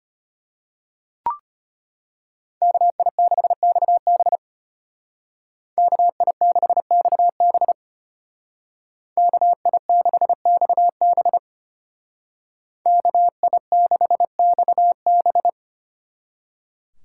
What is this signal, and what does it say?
Morse code, heard as a keyed tone, about 32 wpm, 700 Hz. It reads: KI6XB KI6XB KI6XB KI6XB